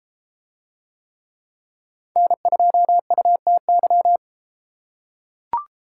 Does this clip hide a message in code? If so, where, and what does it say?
Yes: Morse code from 2.2 to 4.2 s, heard as a keyed tone, about 31 wpm, 700 Hz. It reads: N2UTY